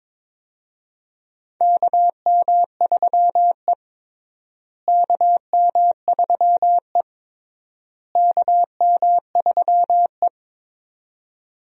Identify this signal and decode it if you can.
Morse code, heard as a keyed tone, about 22 wpm, 700 Hz. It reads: KM3E KM3E KM3E
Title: USA Call Signs - 4 Characters - Repeated 3x - 22wpm